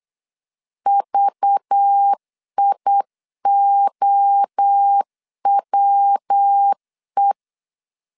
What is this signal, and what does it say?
Morse code, heard as a keyed tone, about 8 wpm, 800 Hz. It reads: VIOWE
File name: LEZIONI / 10 SIGNS / 10A SLASH 8wpm 3452